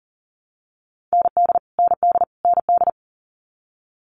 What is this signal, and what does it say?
Morse code, heard as a keyed tone, about 38 wpm, 700 Hz. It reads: ND ND ND
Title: Multi-letter Phonemes Repeated 3x 40wpm